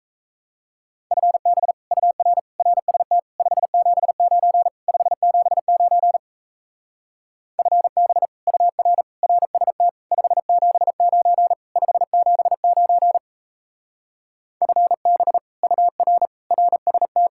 Morse code, heard as a keyed tone, about 35 wpm, 700 Hz. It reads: FB UR RST 579 579 FB UR RST 579 579 FB UR RST
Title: Speed-Racing Common QSO Phrases Encore - 28wpm